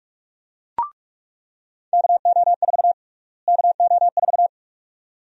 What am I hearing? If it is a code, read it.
Morse code, heard as a keyed tone, about 46 wpm, 700 Hz. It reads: KO4 KO4
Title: USA Call Sign Prefixes - Repeated 2x - 45wpm